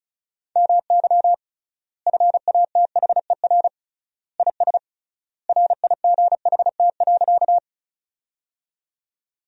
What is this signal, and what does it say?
Morse code, heard as a keyed tone, about 34 wpm, 700 Hz. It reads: MY FATHER IS RIGHT.
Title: 3x Word-Spacing - Sentences from Top 300 Words - 35wpm